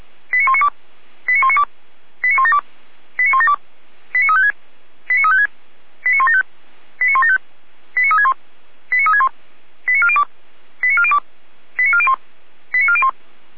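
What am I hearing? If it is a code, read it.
Five-tone selective call (ZVEI), heard as quick runs of five tones, tones of about 70 ms. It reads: 89192 89192 89172 89172 89356 89356 89175 89175 89361 89361 89402 89402 89401 89401
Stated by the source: ZVEI five-tone selective call received by SDR, address 89560.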